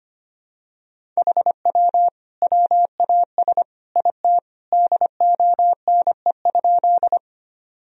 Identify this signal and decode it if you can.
Morse code, heard as a keyed tone, about 25 wpm, 700 Hz. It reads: HW WAS IT DONE?
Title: Sentences from Top 500 Words and Abbreviations 25wpm